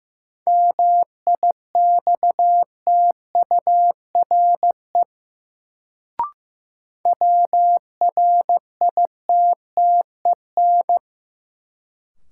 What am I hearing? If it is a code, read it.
Morse code, heard as a keyed tone, about 15 wpm, 700 Hz. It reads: MIXTURE WRITTEN
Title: Sentences from Top 1500 Words - New Word Drill Alt - 15wpm